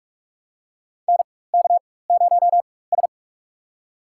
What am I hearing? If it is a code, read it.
Morse code, heard as a keyed tone, about 41 wpm, spaced as for 20 wpm, 700 Hz. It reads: NK0S
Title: USA Call Signs - 4x Character Spacing - 45wpm